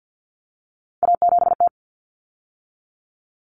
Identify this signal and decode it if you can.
Morse code, heard as a keyed tone, about 50 wpm, 700 Hz. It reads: A7T
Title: Common Cut Numbers - 3-Digit 50wpm